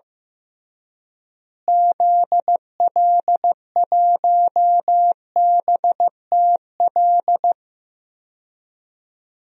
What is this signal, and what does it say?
Morse code, heard as a keyed tone, about 15 wpm, 700 Hz. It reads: ZL1BTL
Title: DXCC-100 Call Signs - 6 Characters - Repeated 3x - 15wpm